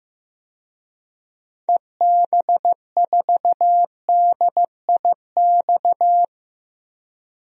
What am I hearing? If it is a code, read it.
Morse code, heard as a keyed tone, about 15 wpm, 700 Hz. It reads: EB4DIX
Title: DXCC-100 Call Signs - 6 Characters - 15wpm